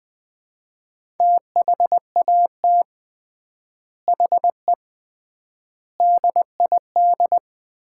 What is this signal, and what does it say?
Morse code, heard as a keyed tone, about 20 wpm, 700 Hz. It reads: THAT HE DID
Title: 3x Word-Spacing - Sets of 4 Words - 20wpm